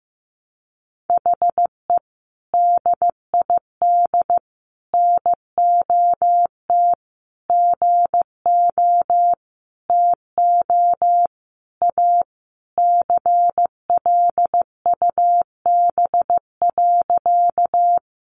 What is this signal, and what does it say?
Morse code, heard as a keyed tone, about 14 wpm, 700 Hz. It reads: HE DID NOT GO TO A CLUB.
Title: Sentences from Top 1200 Words 15wpm